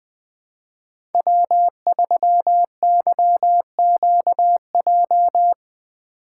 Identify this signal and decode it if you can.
Morse code, heard as a keyed tone, about 20 wpm, 700 Hz. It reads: W3YQJ
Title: Code Groups - 5 Letter:Numbers - 20wpm